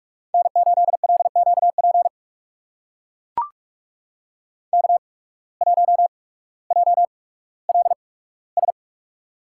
Morse code, heard as a keyed tone, about 44 wpm, spaced as for 19 wpm, 700 Hz. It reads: N8LQP K1JLS
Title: USA Call Signs - 5 Characters - 8x Character Spacing - 45wpm